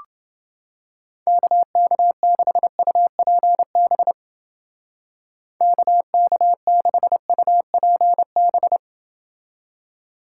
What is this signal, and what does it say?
Morse code, heard as a keyed tone, about 31 wpm, 700 Hz. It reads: KK6UPB KK6UPB